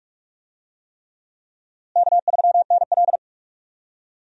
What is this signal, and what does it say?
Morse code, heard as a keyed tone, about 41 wpm, 700 Hz. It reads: K3NL
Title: USA Call Signs - 4 Characters - Repeated 3x - 45wpm